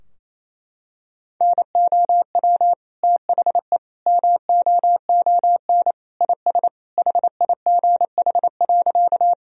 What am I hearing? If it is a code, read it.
Morse code, heard as a keyed tone, about 28 wpm, 700 Hz. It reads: NOW THE MOON IS HIGH.
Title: Sentences from Top 1000 Words - Review Part 3 - 28wpm